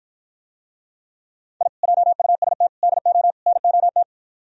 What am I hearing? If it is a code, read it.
Morse code, heard as a keyed tone, about 56 wpm, 700 Hz. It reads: I JUST DO NOT